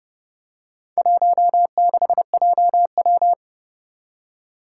Morse code, heard as a keyed tone, about 30 wpm, 690 Hz. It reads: 16JW
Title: Four Challenging Letter-Number Combinations 30wpm